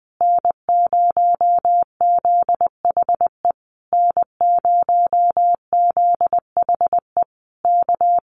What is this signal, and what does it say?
Morse code, heard as a keyed tone, about 19 wpm, 700 Hz. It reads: N0ZHE N0ZHE K
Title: Common QSO Phrases Encore 20wpm